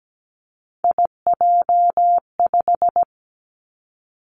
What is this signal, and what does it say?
Morse code, heard as a keyed tone, about 17 wpm, 700 Hz. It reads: IJ5
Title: Three Challenging Letter-Number Combinations 17wpm